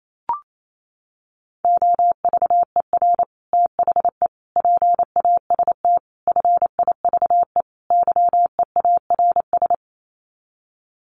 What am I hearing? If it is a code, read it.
Morse code, heard as a keyed tone, about 28 wpm, 700 Hz. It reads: OVER THE PAST FIVE YEARS
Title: Sets of 5 Words Encore 28wpm